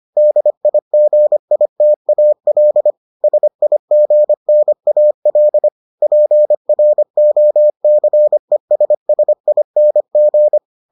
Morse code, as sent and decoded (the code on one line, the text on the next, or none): -.. .. --. .. - .- .-.. / ... .. --. -. .- .-.. / .--. .-. --- -.-. ... ... .. -. --.
DIGITAL SIGNAL PROCESSING